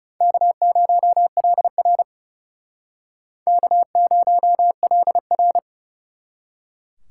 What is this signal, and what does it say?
Morse code, heard as a keyed tone, about 40 wpm, 700 Hz. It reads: K0LR K0LR